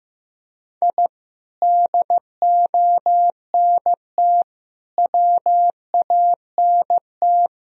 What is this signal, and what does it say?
Morse code, heard as a keyed tone, about 16 wpm, 700 Hz. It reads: I DONT WANT
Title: Sentences from Top 1000 Words - Review Part 3 - 15wpm